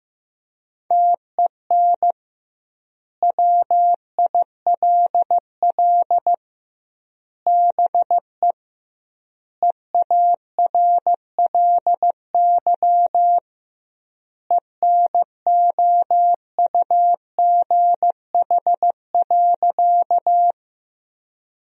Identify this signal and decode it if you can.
Morse code, heard as a keyed tone, about 15 wpm, 700 Hz. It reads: TEN WILL BE EARLY ENOUGH.
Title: 2x Word-Spacing - Sentences from Top 400 Words - 15wpm